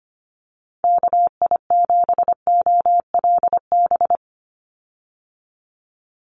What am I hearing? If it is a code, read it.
Morse code, heard as a keyed tone, about 25 wpm, 700 Hz. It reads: KI7OLB
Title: USA Call Signs 25wpm